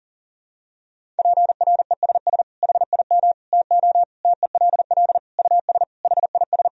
Morse code, heard as a keyed tone, about 40 wpm, 700 Hz. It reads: PRESS HIM TO TELL US HIS
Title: Sentences from Top 1000 Words - Review Part 1 - 40wpm